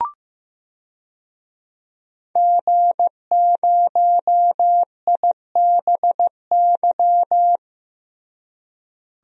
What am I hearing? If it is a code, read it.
Morse code, heard as a keyed tone, about 15 wpm, 700 Hz. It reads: G0IBY
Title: DXCC-100 Call Signs - 5 Characters - Repeated 2x - 15wpm